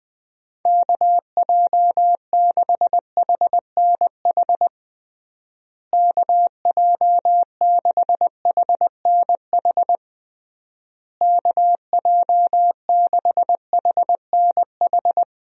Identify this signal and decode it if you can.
Morse code, heard as a keyed tone, about 21 wpm, 700 Hz. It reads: KJ6HNH KJ6HNH KJ6HNH
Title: USA Call Signs - 6 Characters - Repeated 3x - 20wpm